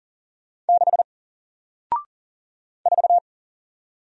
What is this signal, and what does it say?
Morse code, heard as a keyed tone, about 39 wpm, 700 Hz. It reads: 6 4
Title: Single Character (Letter:Number:Punctuation:Prosign) Alt 40wpm